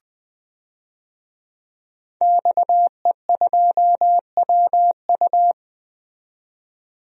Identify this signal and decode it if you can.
Morse code, heard as a keyed tone, about 20 wpm, 700 Hz. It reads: XE2WU
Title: DXCC-100 Call Signs - 5 Characters - 20wpm